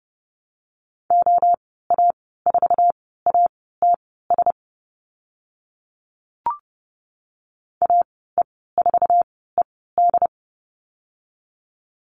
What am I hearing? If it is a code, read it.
Morse code, heard as a keyed tone, about 30 wpm, spaced as for 17 wpm, 700 Hz. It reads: OA4ATS AE4ED
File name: Lesson 14g - Number 2 - Call Signs 30wpm